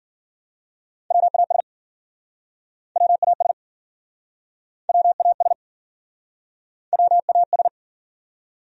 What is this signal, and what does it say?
Morse code, heard as a keyed tone, about 55 wpm, 700 Hz. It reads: WAS WAS WAS WAS